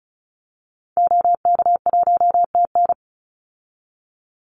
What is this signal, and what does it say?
Morse code, heard as a keyed tone, about 34 wpm, 700 Hz. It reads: OK1TN
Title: Call Signs Repeated 3x - 35wpm